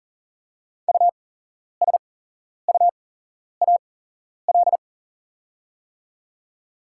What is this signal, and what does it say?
Morse code, heard as a keyed tone, about 40 wpm, spaced as for 11 wpm, 700 Hz. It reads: USUAL
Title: 8x Character Spacing - Top 500 Words - 40wpm